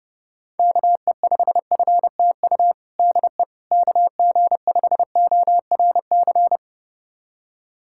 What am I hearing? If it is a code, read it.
Morse code, heard as a keyed tone, about 30 wpm, 700 Hz. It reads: KE5FTU DE KG5ORC
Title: USA Callsign DE Callsign 30wpm